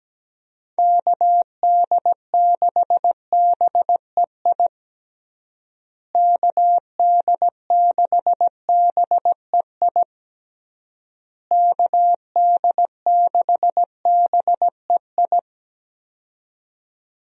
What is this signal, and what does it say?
Morse code, heard as a keyed tone, about 17 wpm, 700 Hz. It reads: KD6BEI KD6BEI KD6BEI
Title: USA Call Signs - 6 Characters - Repeated 3x - 17wpm